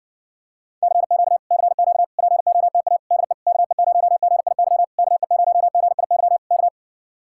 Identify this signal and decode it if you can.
Morse code, heard as a keyed tone, about 69 wpm, 700 Hz. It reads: CQ CQ POTA DE KE0GIY KE0GIY K